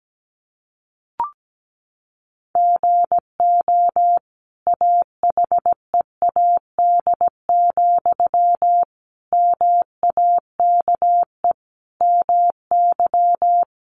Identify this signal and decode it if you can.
Morse code, heard as a keyed tone, about 17 wpm, 700 Hz. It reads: GO AHEAD, MAKE MY
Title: Sentences from Top 1100 Words 17wpm